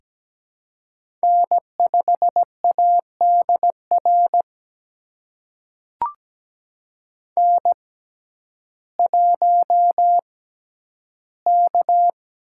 Morse code, heard as a keyed tone, about 17 wpm, spaced as for 10 wpm, 700 Hz. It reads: N5ADR N1K
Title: USA Call Signs - 5 Characters - 6x Character Spacing - 17wpm